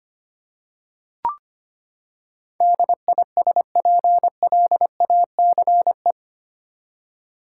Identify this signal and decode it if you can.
Morse code, heard as a keyed tone, about 25 wpm, 700 Hz. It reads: DISPLACE
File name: Eight Letter Words - 25wpm